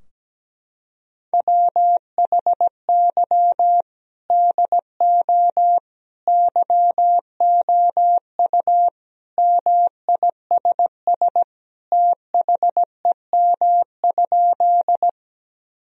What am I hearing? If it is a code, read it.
Morse code, heard as a keyed tone, about 17 wpm, 700 Hz. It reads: WHY DO YOU MISS THEM?
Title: Sentences from Top 1000 Words - Review Part 3 - 17wpm